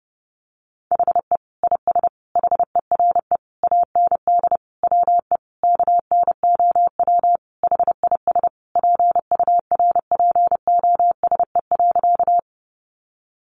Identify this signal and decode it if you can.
Morse code, heard as a keyed tone, about 30 wpm, 700 Hz. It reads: HE IS HERE AND WE KNOW HIS PURPOSE.
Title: Sentences from Top 800 Words 30wpm